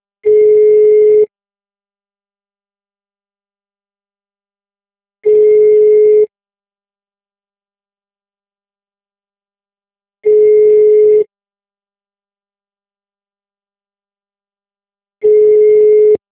Ringback tone (Europe, 425 Hz).
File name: out- -2009- - - .26285